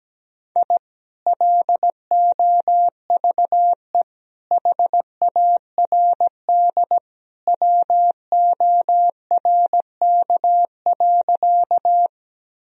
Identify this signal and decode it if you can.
Morse code, heard as a keyed tone, about 17 wpm, 700 Hz. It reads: I LOVE HARD WORK.
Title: Sentences from Top 1000 Words - Review Part 1 - 17wpm